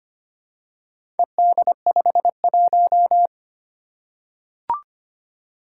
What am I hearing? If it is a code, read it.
Morse code, heard as a keyed tone, about 25 wpm, 700 Hz. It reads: ED51
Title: Four Challenging Letter-Number Combinations 25wpm